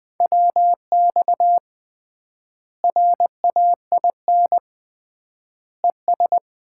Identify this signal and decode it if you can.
Morse code, heard as a keyed tone, about 20 wpm, 700 Hz. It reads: WX RAIN ES
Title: Common QSO Phrases - 3x Word-Spacing - 20wpm